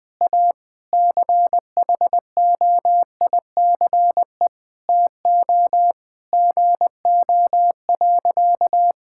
Morse code, heard as a keyed tone, about 21 wpm, 700 Hz. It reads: A CHOICE TO GO.
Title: Sentences with 4-Word Phrases - 20wpm